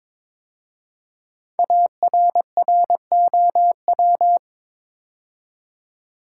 Morse code, heard as a keyed tone, about 22 wpm, 700 Hz. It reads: ARROW
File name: 5 Letter Words - First 3 Letters Given - 22wpm